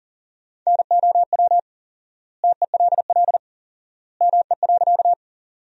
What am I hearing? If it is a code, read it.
Morse code, heard as a keyed tone, about 40 wpm, 700 Hz. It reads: NOW TELL ME.
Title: 4x Word-Spacing - Sentences from Top 200 Words - 40wpm